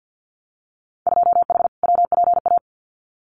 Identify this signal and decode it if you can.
Morse code, heard as a keyed tone, about 50 wpm, 700 Hz. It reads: 2H WPA
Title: Winter Field Day Exchanges 50wpm